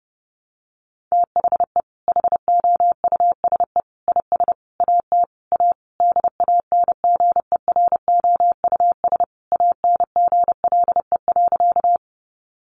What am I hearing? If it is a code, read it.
Morse code, heard as a keyed tone, about 30 wpm, 700 Hz. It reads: THE HOUSE IS AT A DANGEROUS ANGLE.
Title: Sentences from Top 1500 Words 30wpm